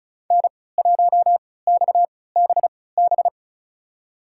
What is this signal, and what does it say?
Morse code, heard as a keyed tone, about 36 wpm, spaced as for 20 wpm, 700 Hz. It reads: N1XBB
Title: USA Call Signs - 5 Characters - 3x Character Spacing - 35wpm